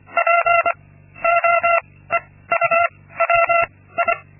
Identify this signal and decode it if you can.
Morse code, heard as a keyed tone, about 24 wpm, 680 Hz. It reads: P O E U W I
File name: Linksmakalnis-2